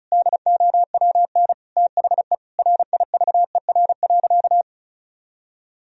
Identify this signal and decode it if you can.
Morse code, heard as a keyed tone, about 34 wpm, 700 Hz. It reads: DOWN THE RIVER.
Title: Sentences from Top 1000 Words - Review Part 1 - 35wpm